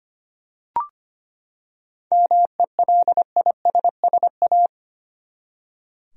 Morse code, heard as a keyed tone, about 25 wpm, 700 Hz. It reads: MELISSA
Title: USA Common Names - 25wpm